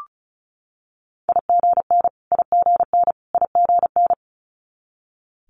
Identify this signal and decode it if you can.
Morse code, heard as a keyed tone, about 36 wpm, 700 Hz. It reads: IGN IGN IGN
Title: Multi-letter Phonemes Repeated 3x 35wpm